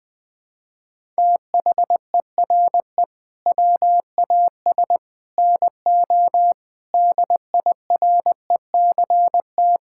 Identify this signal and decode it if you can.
Morse code, heard as a keyed tone, about 20 wpm, 700 Hz. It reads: THERE WAS NO DIRECT
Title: Sentences from Top 1000 Words - Review Part 6 - 20wpm